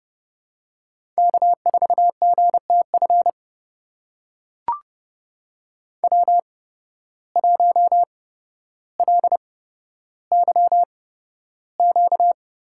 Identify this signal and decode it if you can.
Morse code, heard as a keyed tone, about 30 wpm, spaced as for 13 wpm, 700 Hz. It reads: K4GTF W1LYQ
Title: USA Call Signs - 5 Characters - 8x Character Spacing - 30wpm